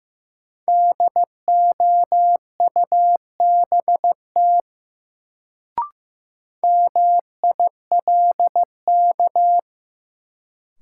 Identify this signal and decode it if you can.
Morse code, heard as a keyed tone, about 15 wpm, 700 Hz. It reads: DOUBT MILK
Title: Sentences from Top 1500 Words - New Word Drill Alt - 15wpm